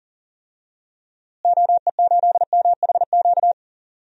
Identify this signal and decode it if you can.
Morse code, heard as a keyed tone, about 40 wpm, 700 Hz. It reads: OE8MHQ
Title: DXCC-100 Call Signs - 6 Characters - Repeated 2x - 40wpm